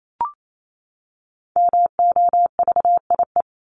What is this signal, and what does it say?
Morse code, heard as a keyed tone, about 28 wpm, 700 Hz. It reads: MOVIE
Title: Top 501-750 Words 28wpm